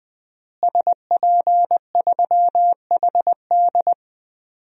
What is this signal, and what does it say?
Morse code, heard as a keyed tone, about 20 wpm, 700 Hz. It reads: SP3HD
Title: DXCC-100 Call Signs - 5 Characters - Repeated 3x - 20wpm